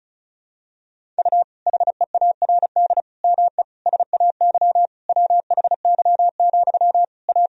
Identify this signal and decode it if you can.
Morse code, heard as a keyed tone, about 34 wpm, 700 Hz. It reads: U HEARD ME SAY WHY, A
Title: Sentences from Top 500 Words and Abbreviations 35wpm